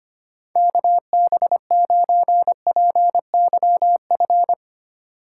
Morse code, heard as a keyed tone, about 25 wpm, 700 Hz. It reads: KB9PYF
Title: USA Call Signs - 6 Characters - 2x Character Spacing - 25wpm